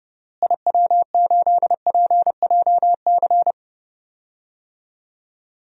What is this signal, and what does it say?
Morse code, heard as a keyed tone, about 30 wpm, 700 Hz. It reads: IW8PJC
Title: DXCC-100 Call Signs - 6 Characters - Repeated 2x - 30wpm